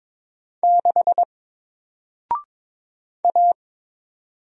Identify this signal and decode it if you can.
Morse code, heard as a keyed tone, about 22 wpm, 700 Hz. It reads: 6 A
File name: Single-Letter-Number-22wpm